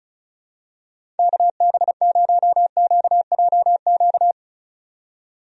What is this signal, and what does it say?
Morse code, heard as a keyed tone, about 34 wpm, 700 Hz. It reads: KB0QJQ
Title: USA Call Signs - 6 Characters - 35wpm